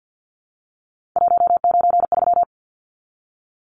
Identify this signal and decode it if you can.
Morse code, heard as a keyed tone, about 50 wpm, 700 Hz. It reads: 193
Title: Numbers - 3-Digit 50wpm